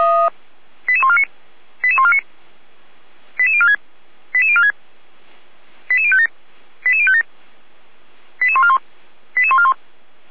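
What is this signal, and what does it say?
Five-tone selective call (ZVEI), heard as quick runs of five tones, tones of about 70 ms. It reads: 88149 88149 88046 88046 88057 88057 88141 88141
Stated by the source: ZVEI five-tone selective call received on an SDR.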